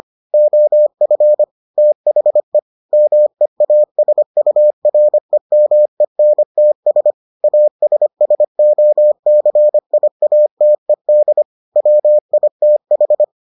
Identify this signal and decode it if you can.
Morse code, heard as a keyed tone, about 25 wpm, 600 Hz. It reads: OF THE MEASUREMENTS ASSOCIATED WITH